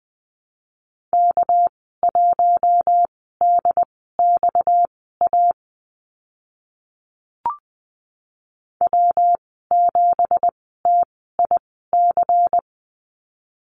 Morse code, heard as a keyed tone, about 20 wpm, spaced as for 15 wpm, 700 Hz. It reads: K1DXA W7TIC
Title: 231 Lesson 37g - Letter x - Call Signs 20wpm